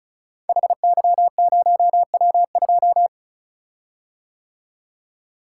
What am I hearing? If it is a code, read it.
Morse code, heard as a keyed tone, about 36 wpm, 700 Hz. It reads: HY0W2
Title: Code Groups - 5 Letter:Numbers - 35wpm